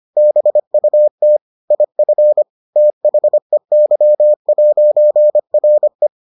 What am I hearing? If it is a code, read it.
Morse code, heard as a keyed tone, about 25 wpm, 600 Hz. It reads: BUT IF THEY'RE